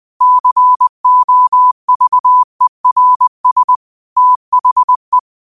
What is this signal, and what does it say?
Morse code, heard as a keyed tone, about 20 wpm, 1000 Hz. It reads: COVERS THE
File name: 20wpm